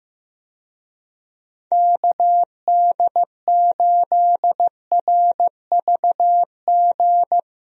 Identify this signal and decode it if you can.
Morse code, heard as a keyed tone, about 15 wpm, 700 Hz. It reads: KD8RVG
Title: USA Call Signs - 6 Characters - 15wpm